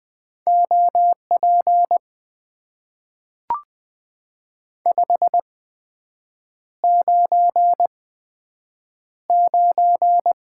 Morse code, heard as a keyed tone, about 20 wpm, spaced as for 7 wpm, 700 Hz. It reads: OP 599
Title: Top 90 QSO Elements - 8x Character Spacing - 20wpm